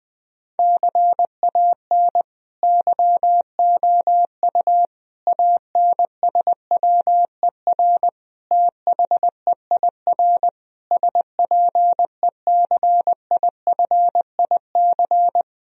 Morse code, heard as a keyed tone, about 20 wpm, 700 Hz. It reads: CAN YOU ANSWER THEIR SPECIFIC
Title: Sentences from Top 1000 Words - Review Part 3 - 20wpm